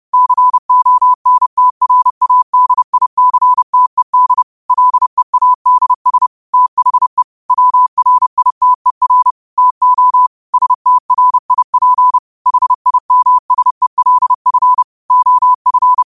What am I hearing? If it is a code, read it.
Morse code, heard as a keyed tone, about 30 wpm, 1000 Hz. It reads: CONTRADICTED LEADS THE WRITER TO STRIP HIMSELF OF
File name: qotdfiles / 30wpm